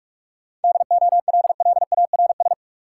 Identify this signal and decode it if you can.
Morse code, heard as a keyed tone, about 46 wpm, 700 Hz. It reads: DOLLARS
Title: Sets of 5 Words - New Word Drill - 45wpm